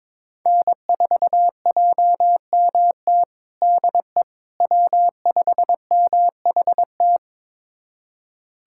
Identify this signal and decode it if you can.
Morse code, heard as a keyed tone, about 22 wpm, 700 Hz. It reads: N4JMT DE W5MHT
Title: USA Callsign DE Callsign - 5 Characters - 22wpm